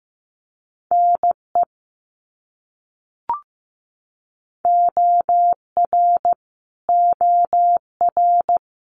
Morse code, heard as a keyed tone, about 15 wpm, 700 Hz. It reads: NE OR OR
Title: US States Repeated 3x 15wpm